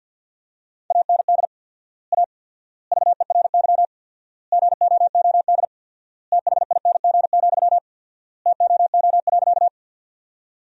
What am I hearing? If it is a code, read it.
Morse code, heard as a keyed tone, about 52 wpm, 700 Hz. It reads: AND A VERY GOOD THING, TOO.
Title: 4x Word-Spacing - Sentences from Top 200 Words - 50wpm